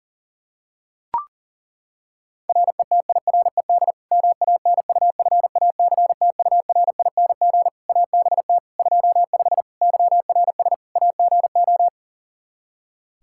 Morse code, heard as a keyed tone, about 37 wpm, 700 Hz. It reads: RETIRED MANUFACTURING ABT 25 YRS AGO
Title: K1USN SST QSO Practice 40wpm